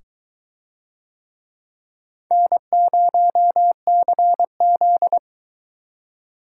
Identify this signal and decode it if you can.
Morse code, heard as a keyed tone, about 22 wpm, 700 Hz. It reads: N0CZ